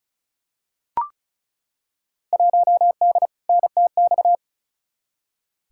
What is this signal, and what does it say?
Morse code, heard as a keyed tone, about 36 wpm, 700 Hz. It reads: 1D NTX
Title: Field Day Exchanges - 35wpm